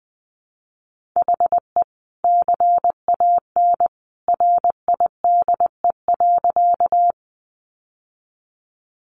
Morse code, heard as a keyed tone, about 20 wpm, 700 Hz. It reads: HE CAN RIDE.
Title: Sentences from Top 1200 Words 20wpm